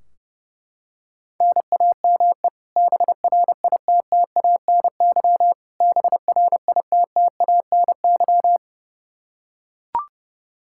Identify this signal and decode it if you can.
Morse code, heard as a keyed tone, about 32 wpm, 700 Hz. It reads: NAME BRITTANY BRITTANY
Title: Practice QSO - Exchange 1 - Beginning - 30wpm